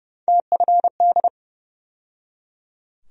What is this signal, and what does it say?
Morse code, heard as a keyed tone, about 30 wpm, 700 Hz. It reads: TFD
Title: USA Call Sign Postfixes - 3 Letters 30wpm